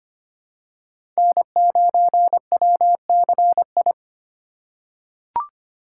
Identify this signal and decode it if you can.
Morse code, heard as a keyed tone, about 25 wpm, 700 Hz. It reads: N9WCI
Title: USA Call Signs - 3x Character Spacing - 25wpm